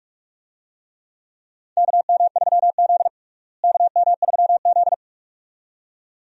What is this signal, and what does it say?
Morse code, heard as a keyed tone, about 41 wpm, 700 Hz. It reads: KM3Z KM3Z
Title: USA Call Signs - 4 Characters - Repeated 2x - 45wpm